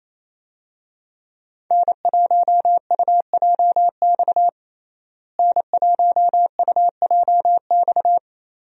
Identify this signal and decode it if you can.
Morse code, heard as a keyed tone, about 28 wpm, 700 Hz. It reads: N1UJX N1UJX
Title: USA Call Signs - 5 Characters - Repeated 2x - 28wpm